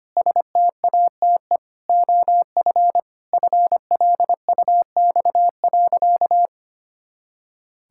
Morse code, heard as a keyed tone, about 25 wpm, 700 Hz. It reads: STATE OF FLUX.
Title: Pangram Challenge v2 - 25wpm